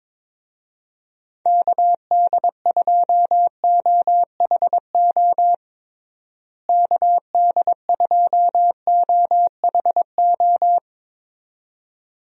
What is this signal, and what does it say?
Morse code, heard as a keyed tone, about 21 wpm, 700 Hz. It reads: KD2OHO KD2OHO
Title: USA Call Signs - 6 Characters - Repeated 2x - 22wpm